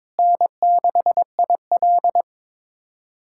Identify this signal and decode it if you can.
Morse code, heard as a keyed tone, about 21 wpm, 700 Hz. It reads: N6IL
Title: USA Callsign DE Callsign - 4 Characters - 22wpm